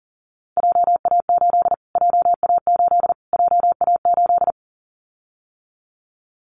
Morse code, heard as a keyed tone, about 37 wpm, 700 Hz. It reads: JA8 JA8 JA8
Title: Callsign Prefixes for Top 25 DXCC - Repeated 3x - 40wpm